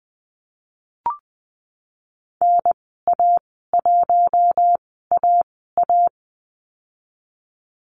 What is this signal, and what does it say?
Morse code, heard as a keyed tone, about 20 wpm, spaced as for 14 wpm, 700 Hz. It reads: NA1AA
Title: 055 Lesson 10g - Letter r - Call Signs 20wpm